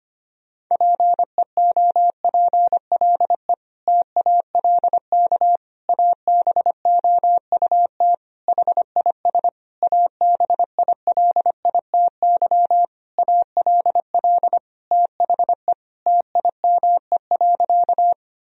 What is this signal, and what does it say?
Morse code, heard as a keyed tone, about 25 wpm, 700 Hz. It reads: PEOPLE TALK ABOUT HIS ABILITY ALL THE TIME.
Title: Sentences from Top 1000 Words - Review Part 3 - 25wpm